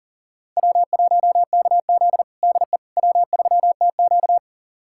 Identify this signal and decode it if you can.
Morse code, heard as a keyed tone, about 40 wpm, 700 Hz. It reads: W1KZ DE W3TQ
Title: USA Callsign DE Callsign - 4 Characters - 40wpm